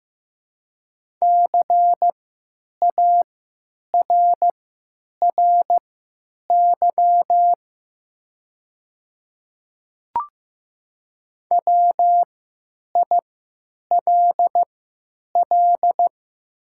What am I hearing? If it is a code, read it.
Morse code, heard as a keyed tone, about 16 wpm, spaced as for 9 wpm, 700 Hz. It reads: CARRY WILL
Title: 3x Character Spacing - Top 300 Words - 15wpm